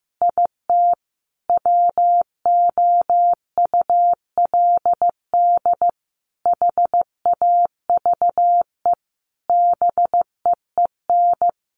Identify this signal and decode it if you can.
Morse code, heard as a keyed tone, about 15 wpm, 700 Hz. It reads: IT WOULD HAVE BEEN
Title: Sentences from Top 1500 Words 15wpm